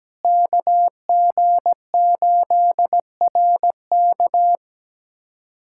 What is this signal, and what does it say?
Morse code, heard as a keyed tone, about 17 wpm, 700 Hz. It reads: KG8RK
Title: USA Callsign DE Callsign 17wpm